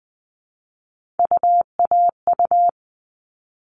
Morse code, heard as a keyed tone, about 20 wpm, 700 Hz. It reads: UAU
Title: All Cut Numbers - 3-Digit 20wpm